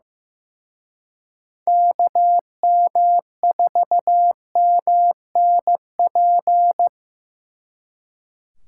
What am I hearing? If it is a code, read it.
Morse code, heard as a keyed tone, about 18 wpm, 700 Hz. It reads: KM4MNP
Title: Speed Racing - USA Call Signs 15wpm